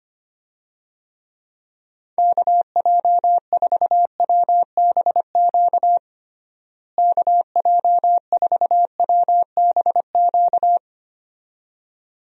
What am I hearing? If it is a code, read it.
Morse code, heard as a keyed tone, about 25 wpm, 700 Hz. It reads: KJ4WBQ KJ4WBQ
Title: USA Call Signs - 6 Characters - Repeated 2x - 25wpm